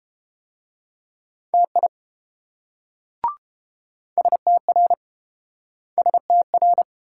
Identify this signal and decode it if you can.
Morse code, heard as a keyed tone, about 30 wpm, 700 Hz. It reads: TI STR STR